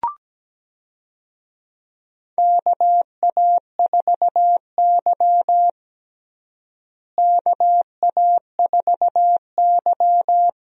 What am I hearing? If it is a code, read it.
Morse code, heard as a keyed tone, about 17 wpm, 700 Hz. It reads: KA4Y KA4Y
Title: USA Call Signs - 4 Characters - Repeated 2x - 17wpm